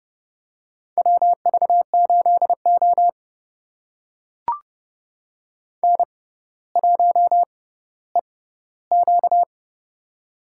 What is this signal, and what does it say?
Morse code, heard as a keyed tone, about 30 wpm, spaced as for 15 wpm, 700 Hz. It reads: WV8O N1EQ
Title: USA Call Signs - 4 Characters - 6x Character Spacing - 30wpm